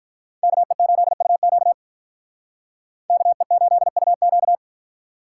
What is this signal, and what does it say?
Morse code, heard as a keyed tone, about 46 wpm, 700 Hz. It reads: KE8UQ KE8UQ